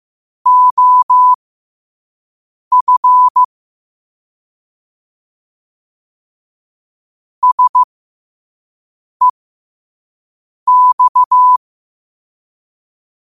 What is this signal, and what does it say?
Morse code, heard as a keyed tone, about 15 wpm, spaced as for 5 wpm, 1000 Hz. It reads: OF SEX